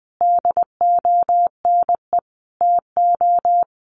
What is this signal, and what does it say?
Morse code, heard as a keyed tone, about 20 wpm, 700 Hz. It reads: DONE TO
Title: A Sound Of Thunder 20wpm